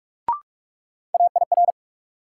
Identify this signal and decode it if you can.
Morse code, heard as a keyed tone, about 44 wpm, 700 Hz. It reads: AIR
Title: Sentences from Top 200 Words - New Word Drill - 45wpm